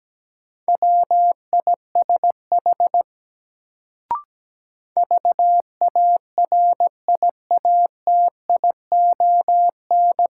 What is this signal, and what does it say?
Morse code, heard as a keyed tone, about 17 wpm, 700 Hz. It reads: WISH VARIATION
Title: Sentences from Top 1600 Words - New Word Drill - 17wpm